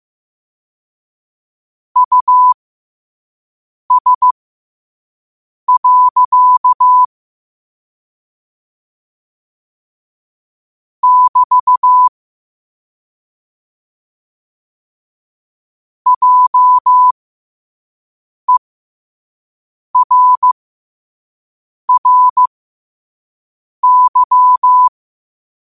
Morse code, heard as a keyed tone, about 15 wpm, spaced as for 5 wpm, 1000 Hz. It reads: US. <BT> JERRY
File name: 2026-03-16-5wpm